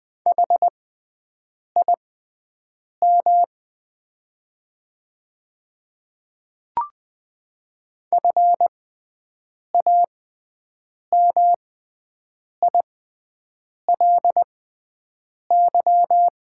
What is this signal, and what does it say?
Morse code, heard as a keyed tone, about 20 wpm, spaced as for 7 wpm, 700 Hz. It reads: HIM FAMILY
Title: 6x Character Spacing - Top 500 Words - 20wpm